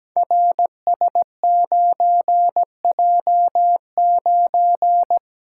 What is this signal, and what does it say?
Morse code, heard as a keyed tone, about 17 wpm, 700 Hz. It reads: RS9J9
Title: Code Groups - 5 Letter:Numbers - 17wpm